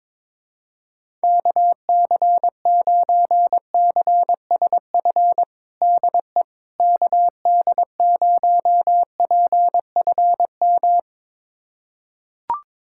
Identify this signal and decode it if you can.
Morse code, heard as a keyed tone, about 22 wpm, 700 Hz. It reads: KC9CSF DE KD0PFM
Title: USA Callsign DE Callsign - 6 Characters - 22wpm